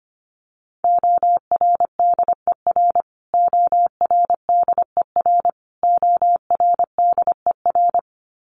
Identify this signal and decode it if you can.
Morse code, heard as a keyed tone, about 25 wpm, 700 Hz. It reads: ORDER ORDER ORDER
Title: Top 500 Words Repeated 25wpm